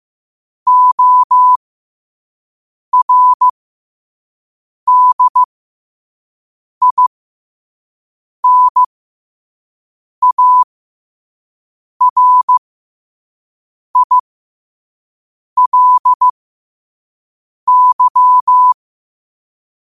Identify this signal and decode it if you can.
Morse code, heard as a keyed tone, about 15 wpm, spaced as for 5 wpm, 1000 Hz. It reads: ORDINARILY
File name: qotdfiles / 5wpm